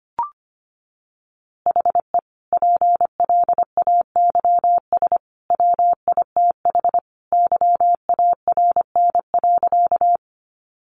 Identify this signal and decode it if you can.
Morse code, heard as a keyed tone, about 25 wpm, 700 Hz. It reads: HE PLAYS WITH YARN.
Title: Easy Phrases 25wpm